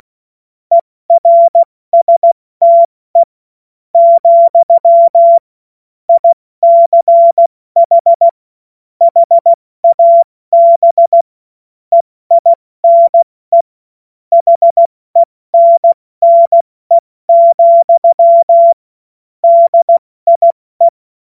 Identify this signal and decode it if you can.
Morse code, heard as a keyed tone, about 16 wpm, spaced as for 14 wpm, 680 Hz. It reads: ERSTE, ICH HAB EINE HENNE, DIE
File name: Morsekurs / Witze / Witz 01. Februar 2022 70 F80